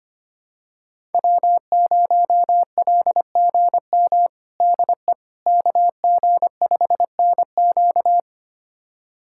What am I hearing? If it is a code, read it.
Morse code, heard as a keyed tone, about 25 wpm, 700 Hz. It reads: W0LGM DE KG5NQ
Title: USA Callsign DE Callsign - 5 Characters - 25wpm